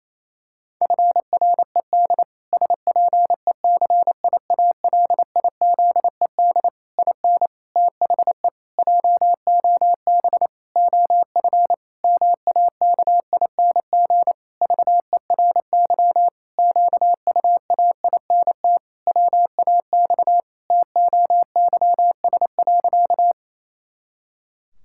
Morse code, heard as a keyed tone, about 28 wpm, 700 Hz. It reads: FRED SPECIALIZED IN THE JOB OF MAKING VERY QUAINT WAX TOYS.
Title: Pangram Challenge v2 - 28wpm